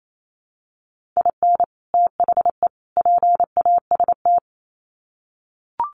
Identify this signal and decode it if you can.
Morse code, heard as a keyed tone, about 28 wpm, 700 Hz. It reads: IN THE PAST
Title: Sets of 3 Words 28wpm